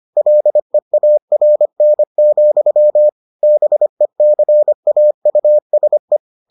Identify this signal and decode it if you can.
Morse code, heard as a keyed tone, about 25 wpm, 600 Hz. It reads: LEARN, BECAUSE